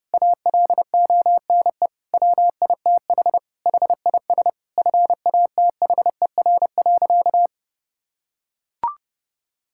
Morse code, heard as a keyed tone, about 30 wpm, 700 Hz. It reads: ALONE WITH HIS FATHER.
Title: Sentences from Top 1000 Words - Review Part 5 - 30wpm